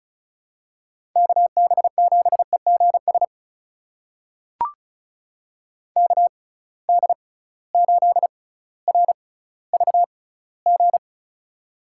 Morse code, heard as a keyed tone, about 36 wpm, spaced as for 18 wpm, 700 Hz. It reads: KB7EGS KD8RVG
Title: USA Call Signs - 6 Characters - 6x Character Spacing - 35wpm